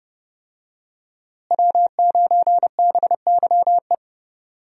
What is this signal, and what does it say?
Morse code, heard as a keyed tone, about 31 wpm, 700 Hz. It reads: W9BYE
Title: USA Call Signs - 5 Characters - Repeated 3x - 30wpm